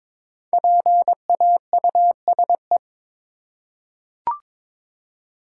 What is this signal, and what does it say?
Morse code, heard as a keyed tone, about 22 wpm, 700 Hz. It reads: PAUSE
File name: 5 Letter Words - First 3 Letters Given - 22wpm